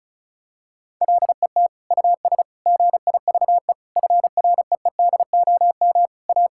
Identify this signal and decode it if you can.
Morse code, heard as a keyed tone, about 36 wpm, 700 Hz. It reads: LET US GIVE FREEDOM A
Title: Sentences from Top 1000 Words - Review Part 5 - 35wpm